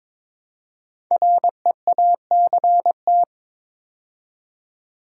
Morse code, heard as a keyed tone, about 22 wpm, 690 Hz. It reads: REACT
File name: 5 Letter Words - First 3 Letters Given - 22wpm